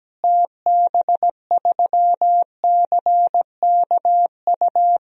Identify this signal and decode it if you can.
Morse code, heard as a keyed tone, about 17 wpm, 700 Hz. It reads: TB3CKU
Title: DXCC-100 Call Signs - 6 Characters - Repeated 2x - 17wpm